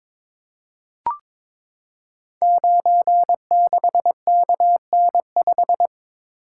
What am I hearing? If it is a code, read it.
Morse code, heard as a keyed tone, about 23 wpm, 700 Hz. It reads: 96KN5
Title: Code Groups - 5 Letter:Numbers - 22wpm